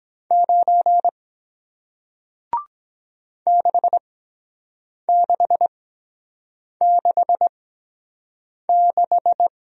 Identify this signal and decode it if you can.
Morse code, heard as a keyed tone, about 23 wpm, spaced as for 7 wpm, 700 Hz. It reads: 9 6666